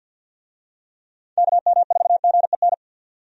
Morse code, heard as a keyed tone, about 45 wpm, 700 Hz. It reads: KM4GEN
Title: USA Call Signs - 6 Characters - 50wpm